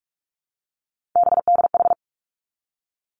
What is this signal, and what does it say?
Morse code, heard as a keyed tone, about 46 wpm, 700 Hz. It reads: BDH
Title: Three Challenging Letter-Number Combinations 45wpm